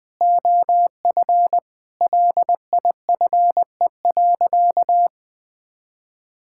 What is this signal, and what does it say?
Morse code, heard as a keyed tone, about 20 wpm, 700 Hz. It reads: OF LIFE.
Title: Sentences from Top 1000 Words - Review Part 6 - 20wpm